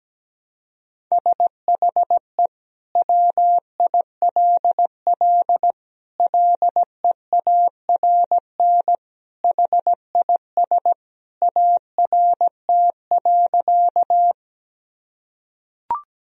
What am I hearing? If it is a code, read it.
Morse code, heard as a keyed tone, about 17 wpm, 700 Hz. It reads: SHE WILL LEARN HIS ART.
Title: Sentences from Top 1000 Words - Review Part 3 - 17wpm